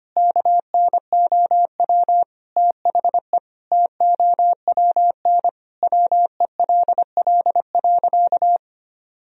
Morse code, heard as a keyed tone, about 25 wpm, 700 Hz. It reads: KNOW THE TOWN WELL.
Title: Sentences from Top 1000 Words - Review Part 2 - 25wpm